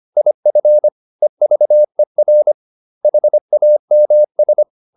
Morse code, heard as a keyed tone, about 25 wpm, 600 Hz. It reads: IF EVER HAMS